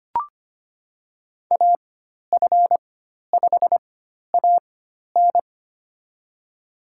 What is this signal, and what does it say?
Morse code, heard as a keyed tone, about 25 wpm, spaced as for 12 wpm, 700 Hz. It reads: AF5AN
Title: USA Call Signs - 4x Character Spacing - 25wpm